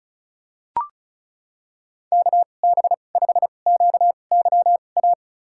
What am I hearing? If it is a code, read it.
Morse code, heard as a keyed tone, about 36 wpm, spaced as for 25 wpm, 700 Hz. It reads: KB5QYA
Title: USA Call Signs - 6 Characters - 2x Character Spacing - 35wpm